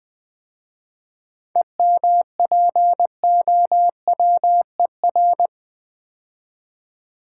Morse code, heard as a keyed tone, about 20 wpm, 700 Hz. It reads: EMPOWER
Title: 7 Letter Words - First 3 Letters Given - 20wpm